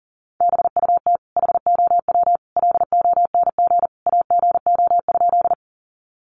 Morse code, heard as a keyed tone, about 40 wpm, 700 Hz. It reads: BUT HOW LONG AGO?
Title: Sentences from Top 500 Words 40wpm